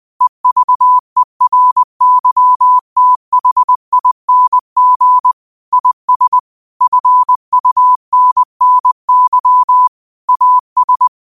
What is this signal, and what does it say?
Morse code, heard as a keyed tone, about 20 wpm, 1000 Hz. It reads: EVERYTHING IS FUNNY AS
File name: qotdfiles / 20wpm